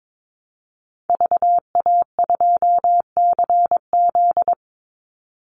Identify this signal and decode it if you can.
Morse code, heard as a keyed tone, about 22 wpm, 700 Hz. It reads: VA2CZ
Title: Call Signs Repeated 3x - 22wpm